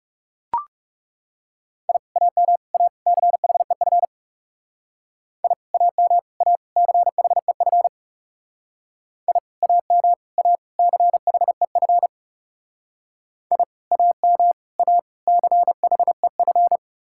Morse code, heard as a keyed tone, about 35 wpm, 700 Hz. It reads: I AM A CHEF I AM A CHEF I AM A CHEF I AM A CHEF
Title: Speed-Racing Common QSO Phrases Encore - 30wpm